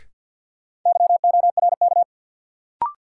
Morse code, heard as a keyed tone, about 51 wpm, 700 Hz. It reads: YORK
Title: Sets of 2 Words - New Word Drill - 50wpm